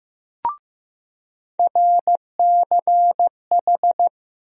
Morse code, heard as a keyed tone, about 16 wpm, 700 Hz. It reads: RCH
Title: USA Call Sign Postfixes - 3 Letters 15wpm